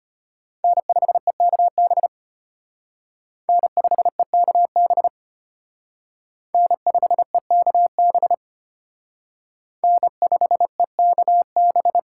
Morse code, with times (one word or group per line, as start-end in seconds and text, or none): 0.6-2.1 N5EKB
3.5-5.1 N5EKB
6.5-8.3 N5EKB
9.8-12.0 N5EKB